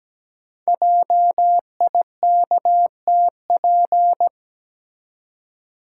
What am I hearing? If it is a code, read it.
Morse code, heard as a keyed tone, about 17 wpm, 700 Hz. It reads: JIKTP
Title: Code Groups - 5 Letters - 17wpm